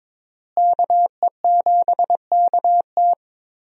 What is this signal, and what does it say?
Morse code, heard as a keyed tone, about 22 wpm, 700 Hz. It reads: KE7KT
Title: USA Call Signs - 5 Characters - Repeated 3x - 22wpm